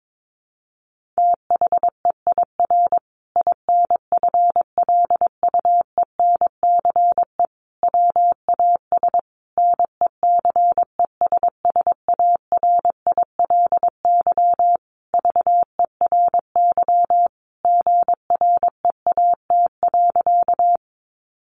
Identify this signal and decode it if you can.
Morse code, heard as a keyed tone, about 22 wpm, 700 Hz. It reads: THEIR INFLUENCE WAS NECESSARILY VERY GREAT.
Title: Sentences from Top 1400 Words 22wpm